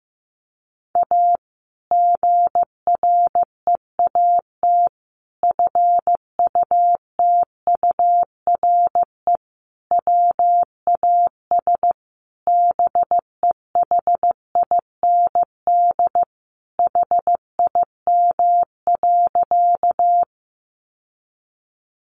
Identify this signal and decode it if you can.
Morse code, heard as a keyed tone, about 15 wpm, 700 Hz. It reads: A GREAT FUTURE WAS BEHIND HIM.
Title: Sentences from Top 700 Words 15wpm